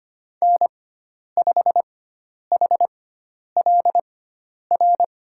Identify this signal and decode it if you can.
Morse code, heard as a keyed tone, about 25 wpm, spaced as for 10 wpm, 700 Hz. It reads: N5HLR
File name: Lesson 16g - Letter c - Call Signs 25wpm